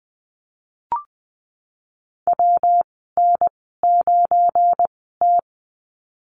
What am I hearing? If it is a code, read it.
Morse code, heard as a keyed tone, about 20 wpm, spaced as for 15 wpm, 700 Hz. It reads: WN9T
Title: 186 Lesson 30g - Letter b - Call Signs 20wpm